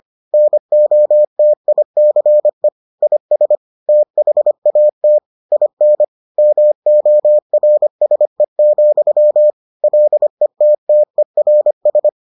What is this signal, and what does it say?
Morse code, heard as a keyed tone, about 25 wpm, 600 Hz. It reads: NOTICE IS THAT IN MORSE, LETTERS